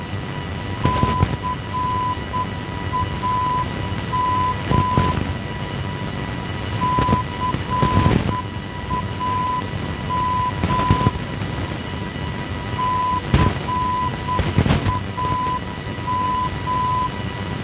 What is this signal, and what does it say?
Morse code, heard as a keyed tone, about 8 wpm, 1000 Hz, with noise. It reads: CAM CAM CAM
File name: CAM NDB 1988